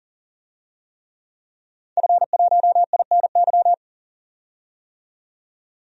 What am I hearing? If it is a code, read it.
Morse code, heard as a keyed tone, about 40 wpm, 700 Hz. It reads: F1INY
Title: DXCC-100 Call Signs - 5 Characters - 40wpm